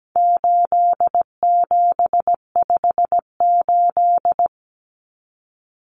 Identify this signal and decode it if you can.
Morse code, heard as a keyed tone, about 17 wpm, 700 Hz. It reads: 8758
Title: Numbers - 4-Digit 17wpm